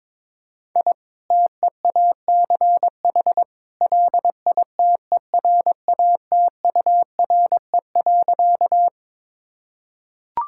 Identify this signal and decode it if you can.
Morse code, heard as a keyed tone, about 22 wpm, 700 Hz. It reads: I TEACH LITERATURE.
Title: Sentences from Top 1000 Words - Review Part 6 - 22wpm